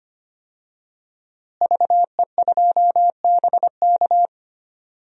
Morse code, heard as a keyed tone, about 25 wpm, 700 Hz. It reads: VE2BK
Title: DXCC-100 Call Signs - 5 Characters - Repeated 3x - 25wpm